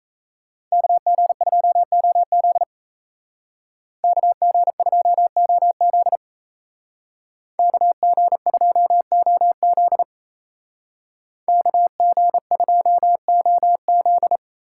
Morse code, heard as a keyed tone, about 37 wpm, 700 Hz. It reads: KG2OZ KG2OZ KG2OZ KG2OZ